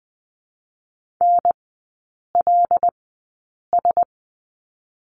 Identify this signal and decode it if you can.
Morse code, heard as a keyed tone, about 19 wpm, spaced as for 8 wpm, 690 Hz. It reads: NLS
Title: 087 Lesson 15e - Number 5 - Three Characters 20wpm